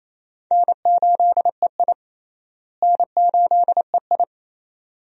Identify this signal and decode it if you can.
Morse code, heard as a keyed tone, about 28 wpm, 700 Hz. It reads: N8EI N8EI
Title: USA Call Signs - 4 Characters - Repeated 2x - 28wpm